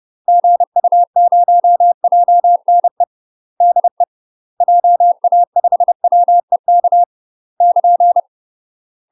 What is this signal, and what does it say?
Morse code, heard as a keyed tone, about 30 wpm, 700 Hz. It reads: GU0JNE DE JA5WEK <KN>
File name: UK MASTER QSOs TRAINING / 30WPM / UK MASTER QSOs 0041